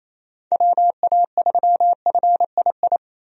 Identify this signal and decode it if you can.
Morse code, heard as a keyed tone, about 28 wpm, 700 Hz. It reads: WA3FII
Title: USA Call Signs - 6 Characters - Repeated 2x - 28wpm